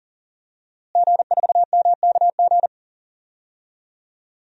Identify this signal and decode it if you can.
Morse code, heard as a keyed tone, about 40 wpm, 700 Hz. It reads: G4MKG
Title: DXCC-100 Call Signs - 5 Characters - Repeated 3x - 40wpm